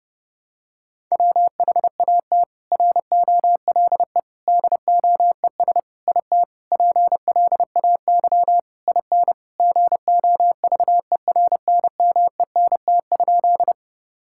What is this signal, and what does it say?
Morse code, heard as a keyed tone, about 32 wpm, 700 Hz. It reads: WHAT ROLE DOES IT PLAY IN GOVERNMENT?
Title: Sentences from Top 1000 Words - Review Part 6 - 30wpm